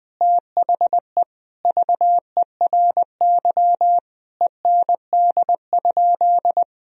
Morse code, heard as a keyed tone, about 20 wpm, 700 Hz. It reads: THE VERY END?
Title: Sentences from Top 1000 Words - Review Part 3 - 20wpm